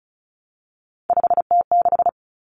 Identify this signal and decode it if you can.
Morse code, heard as a keyed tone, about 34 wpm, 700 Hz. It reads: 5T6
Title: Common Cut Numbers - 3-Digit 35wpm